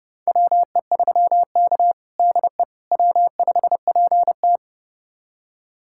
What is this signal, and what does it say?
Morse code, heard as a keyed tone, about 32 wpm, 700 Hz. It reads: WE3K DE W5PT
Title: USA Callsign DE Callsign - 4 Characters - 30wpm